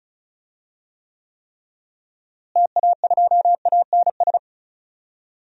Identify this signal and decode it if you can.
Morse code, heard as a keyed tone, about 36 wpm, 700 Hz. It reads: TA2ANS
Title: DXCC-100 Call Signs - 6 Characters - Repeated 3x - 35wpm